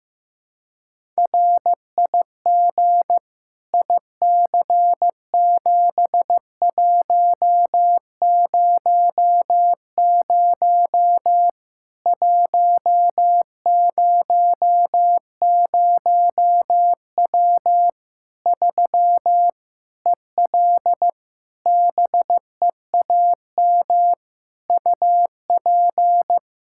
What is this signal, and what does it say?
Morse code, heard as a keyed tone, about 15 wpm, 700 Hz. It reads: RIG IC7100 100W 3 EL BEAM UP
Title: Practice QSO - Exchange 2 - Info - 15wpm